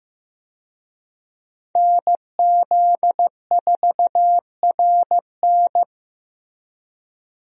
Morse code, heard as a keyed tone, about 15 wpm, 700 Hz. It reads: NZ4RN
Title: USA Call Signs - 5 Characters - Repeated 2x - 15wpm